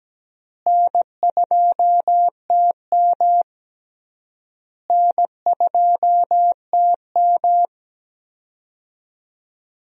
Morse code, heard as a keyed tone, about 17 wpm, 700 Hz. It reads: N2TM N2TM
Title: USA Call Signs - 4 Characters - Repeated 3x - 17wpm